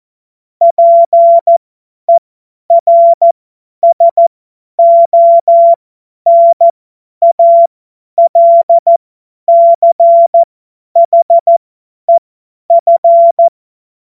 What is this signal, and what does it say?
Morse code, heard as a keyed tone, about 14 wpm, spaced as for 10 wpm, 680 Hz. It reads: PERSONALCHEF
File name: Morsekurs / Witze / Witz 17. Januar 2023 50 F70